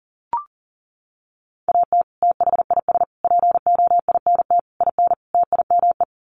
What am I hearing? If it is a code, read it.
Morse code, heard as a keyed tone, about 40 wpm, 700 Hz. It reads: AT THIS POINT IN TIME
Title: Sets of 5 Words Encore 40wpm